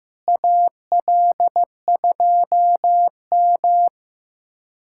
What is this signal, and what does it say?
Morse code, heard as a keyed tone, about 15 wpm, 700 Hz. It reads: AL2M
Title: USA Callsign DE Callsign 15wpm